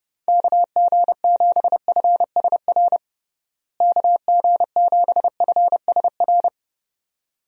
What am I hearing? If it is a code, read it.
Morse code, heard as a keyed tone, about 30 wpm, 700 Hz. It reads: KG7FSR KG7FSR
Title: USA Call Signs - 6 Characters - Repeated 2x - 30wpm